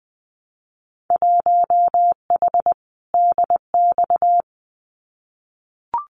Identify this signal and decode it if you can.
Morse code, heard as a keyed tone, about 20 wpm, 700 Hz. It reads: 1H DX
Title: Winter Field Day Exchanges 20wpm